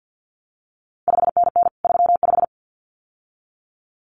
Morse code, heard as a keyed tone, about 50 wpm, 700 Hz. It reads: 5NN 35
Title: CQ WW DX Contest Exchange - 50wpm